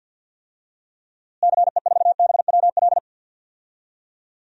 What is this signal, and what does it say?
Morse code, heard as a keyed tone, about 52 wpm, 700 Hz. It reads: CE4BWL
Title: DXCC-100 Call Signs - 6 Characters - Repeated 2x - 50wpm